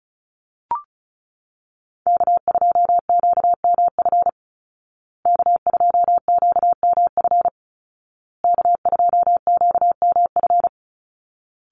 Morse code, heard as a keyed tone, about 34 wpm, 700 Hz. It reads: K2QMF K2QMF K2QMF
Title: Call Signs Repeated 2x - 35wpm